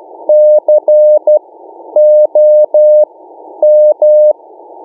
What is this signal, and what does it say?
Morse code, heard as a keyed tone, about 12 wpm, spaced as for 9 wpm, 600 Hz. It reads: COM
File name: mw / 1926 501 ei0cf beacon